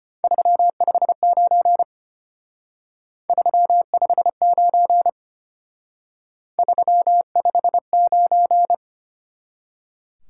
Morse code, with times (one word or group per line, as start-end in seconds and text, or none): 0.2-1.8 359
3.3-5.1 359
6.6-8.7 359